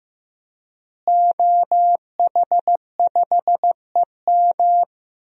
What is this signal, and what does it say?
Morse code, heard as a keyed tone, about 15 wpm, 700 Hz. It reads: OH5EM
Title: DXCC-100 Call Signs - 5 Characters - Repeated 3x - 15wpm